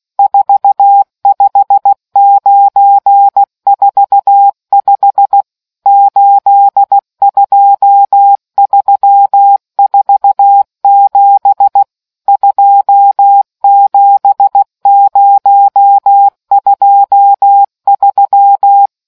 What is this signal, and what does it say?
Morse code, heard as a keyed tone, about 16 wpm, 800 Hz. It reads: 45945 82347 27023